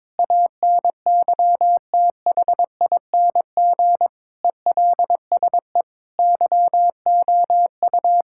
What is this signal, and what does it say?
Morse code, heard as a keyed tone, about 21 wpm, 700 Hz. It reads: ANYTHING ELSE YOU
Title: Sentences from Top 1600 Words 22wpm